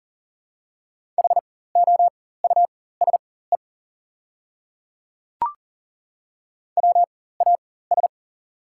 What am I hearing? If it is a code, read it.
Morse code, heard as a keyed tone, about 40 wpm, spaced as for 19 wpm, 700 Hz. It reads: HOUSE WAS
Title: 4x Character Spacing - Top 200 Words - 40wpm